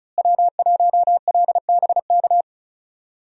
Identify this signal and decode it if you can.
Morse code, heard as a keyed tone, about 36 wpm, 700 Hz. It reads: W1LBK
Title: USA Call Signs - 5 Characters - Repeated 3x - 35wpm